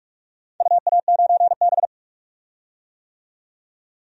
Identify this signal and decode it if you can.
Morse code, heard as a keyed tone, about 46 wpm, 700 Hz. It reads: UA9B
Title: DXCC-100 Call Signs - 4 Characters - 45wpm